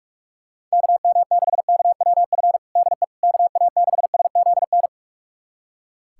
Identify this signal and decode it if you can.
Morse code, heard as a keyed tone, about 43 wpm, 700 Hz. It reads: KM6KWF DE KA6SZN
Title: USA Callsign DE Callsign - 6 Characters - 45wpm